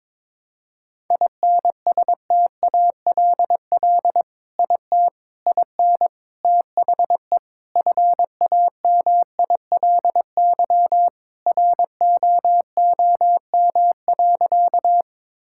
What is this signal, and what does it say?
Morse code, heard as a keyed tone, about 22 wpm, 700 Hz. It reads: INSTALL IT IN THE FAMILY ROOM.
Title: Sentences from Top 1600 Words 22wpm